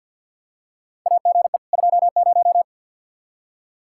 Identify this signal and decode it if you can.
Morse code, heard as a keyed tone, about 48 wpm, 700 Hz. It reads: AGE 20
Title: Practice QSO - Exchange 3 - More Info - 50wpm